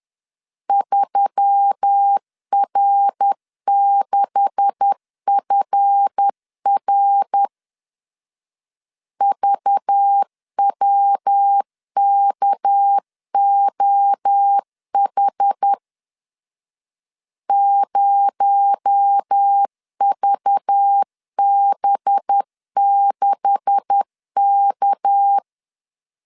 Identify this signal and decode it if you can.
Morse code, heard as a keyed tone, about 10 wpm, 800 Hz. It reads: 3R6FR VWKOH 0VB6K